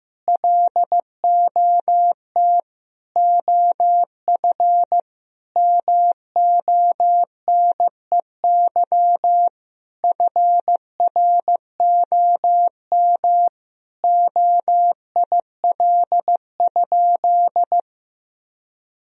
Morse code, heard as a keyed tone, about 15 wpm, 700 Hz. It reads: LOT OF MONEY FROM OIL?
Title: Sentences from Top 1000 Words - Review Part 4 - 15wpm